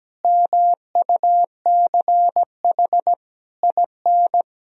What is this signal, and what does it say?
Morse code, heard as a keyed tone, about 17 wpm, 700 Hz. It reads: MUCH IN
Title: Sentences from Top 1000 Words - Review Part 4 - 17wpm